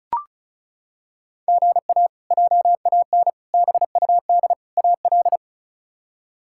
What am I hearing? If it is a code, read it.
Morse code, heard as a keyed tone, about 36 wpm, 700 Hz. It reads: GA JAN BUD AL
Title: K1USN SST QSO Practice - 35wpm